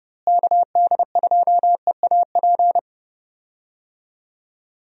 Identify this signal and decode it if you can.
Morse code, heard as a keyed tone, about 30 wpm, 700 Hz. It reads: KD2EAP
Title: USA Call Signs - 6 Characters - Repeated 2x - 30wpm